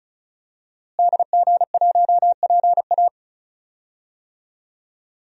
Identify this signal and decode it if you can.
Morse code, heard as a keyed tone, about 36 wpm, 700 Hz. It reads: DG1PA
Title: DXCC-100 Call Signs - 5 Characters - Repeated 2x - 35wpm